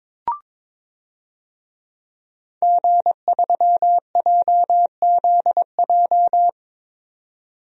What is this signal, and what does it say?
Morse code, heard as a keyed tone, about 22 wpm, 700 Hz. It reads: G3JZJ
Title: DXCC-100 Call Signs - 5 Characters - 22wpm